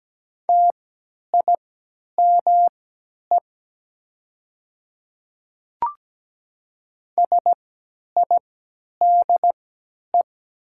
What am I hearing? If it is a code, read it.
Morse code, heard as a keyed tone, about 17 wpm, spaced as for 10 wpm, 690 Hz. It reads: TIME SIDE
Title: 3x Character Spacing - Top 500 Words - 17wpm